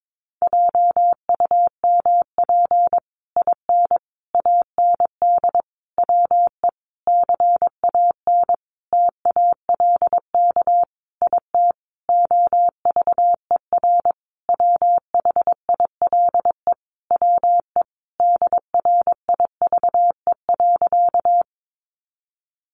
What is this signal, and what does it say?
Morse code, heard as a keyed tone, about 22 wpm, 700 Hz. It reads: JUMP IN AND WE CAN TALK IT OVER WHILE WE DRIVE.
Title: Sentences from Top 1300 Words 22wpm